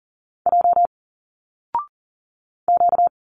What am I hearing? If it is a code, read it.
Morse code, heard as a keyed tone, about 41 wpm, 700 Hz. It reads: J Q
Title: Single Letters 40wpm